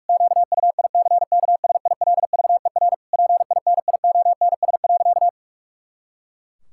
Morse code, heard as a keyed tone, about 46 wpm, 700 Hz. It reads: QUICKSILVER PINIONS.
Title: Pangram Challenge v2 - 45wpm